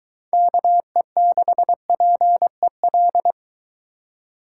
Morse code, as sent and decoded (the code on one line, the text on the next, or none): -.- . -.... .--. .-..
KE6PEL